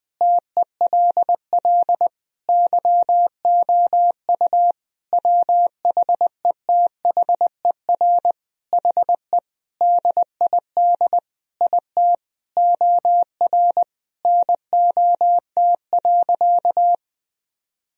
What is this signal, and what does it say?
Morse code, heard as a keyed tone, about 20 wpm, 700 Hz. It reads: TELL YOU WHETHER HE DID IT OR NOT.
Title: Sentences from Top 1000 Words - Review Part 3 - 20wpm